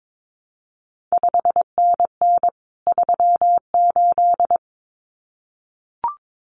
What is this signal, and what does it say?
Morse code, heard as a keyed tone, about 22 wpm, 700 Hz. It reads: 5NN 38
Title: CQ WW DX Contest Exchange Encore - 22wpm